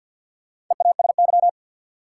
Morse code, heard as a keyed tone, about 52 wpm, 700 Hz. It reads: EASY
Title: Sentences from Top 700 Words - New Word Drill - 50wpm